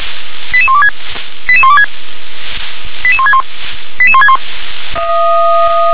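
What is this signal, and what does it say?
Five-tone selective call (ZVEI), heard as quick runs of five tones, tones of about 70 ms. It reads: 88126 88126 88161 88161
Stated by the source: ZVEI five-tone selective call received by SDR.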